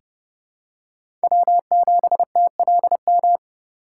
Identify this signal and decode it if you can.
Morse code, heard as a keyed tone, about 32 wpm, 700 Hz. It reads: W7TLM
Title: USA Call Signs - 5 Characters - 3x Character Spacing - 30wpm